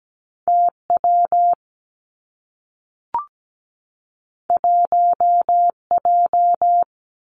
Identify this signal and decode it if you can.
Morse code, heard as a keyed tone, about 17 wpm, 700 Hz. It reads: T W 1 J
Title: Two Challenging Letter-Number Combinations 17wpm